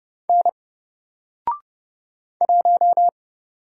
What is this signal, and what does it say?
Morse code, heard as a keyed tone, about 30 wpm, 700 Hz. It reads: N 1
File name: Single-Letter-Number-30wpm